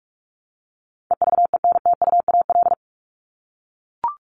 Morse code, heard as a keyed tone, about 44 wpm, 700 Hz. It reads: EVENTUAL
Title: LY Suffix - Top 100 Words with Base Words 45wpm